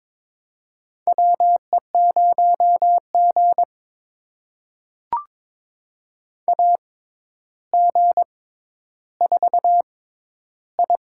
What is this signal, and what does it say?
Morse code, heard as a keyed tone, about 22 wpm, spaced as for 11 wpm, 700 Hz. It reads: WE0G AG4I
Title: USA Call Signs - 4 Characters - 6x Character Spacing - 22wpm